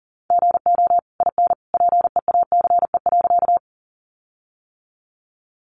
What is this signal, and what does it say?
Morse code, heard as a keyed tone, about 40 wpm, 700 Hz. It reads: GO IN PEACE.
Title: Sentences from Top 1500 Words 40wpm